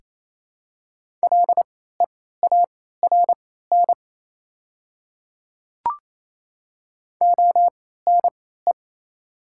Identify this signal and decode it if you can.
Morse code, heard as a keyed tone, about 27 wpm, spaced as for 16 wpm, 700 Hz. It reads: LEARN ONE
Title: 3x Character Spacing - Top 300 Words - 28wpm